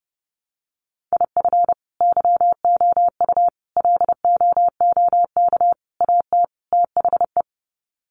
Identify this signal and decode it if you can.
Morse code, heard as a keyed tone, about 30 wpm, 700 Hz. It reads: IF YOU LOOK AT THE
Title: Sets of 5 Words 30wpm